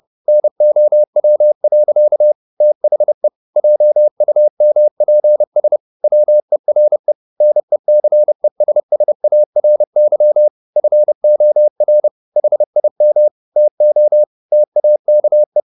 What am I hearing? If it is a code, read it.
Morse code, heard as a keyed tone, about 30 wpm, 600 Hz. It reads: NOW. THE JUMPS WERE NECESSARY FOR HIM TO TAKE